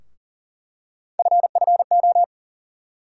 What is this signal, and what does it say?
Morse code, heard as a keyed tone, about 40 wpm, 700 Hz. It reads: FFO
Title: USA Call Sign Postfixes - 3 Letters 40wpm